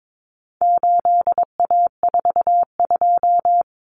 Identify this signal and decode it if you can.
Morse code, heard as a keyed tone, about 22 wpm, 700 Hz. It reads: 8A42
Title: Common Cut Numbers - 4-Digit 22wpm